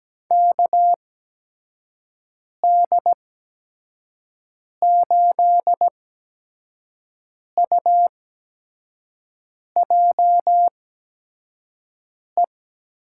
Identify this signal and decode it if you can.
Morse code, heard as a keyed tone, about 17 wpm, spaced as for 5 wpm, 700 Hz. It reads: KD8UJE
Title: USA Call Signs - 6 Characters - 8x Character Spacing - 17wpm